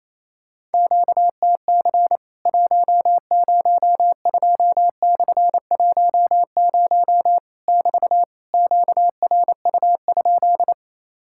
Morse code, heard as a keyed tone, about 28 wpm, 700 Hz. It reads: QTC 102/10 <BT> QRU?
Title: Worked All Europe DX Contest - QTC Exchange - 28wpm